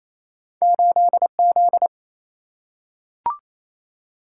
Morse code, heard as a keyed tone, about 28 wpm, 700 Hz. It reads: 8Z
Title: Single-Letter Single-Number 28wpm